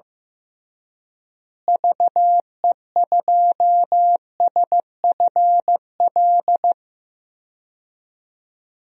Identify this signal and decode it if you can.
Morse code, heard as a keyed tone, about 15 wpm, 700 Hz. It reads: VE2SFL
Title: DXCC-100 Call Signs - 6 Characters - Repeated 3x - 15wpm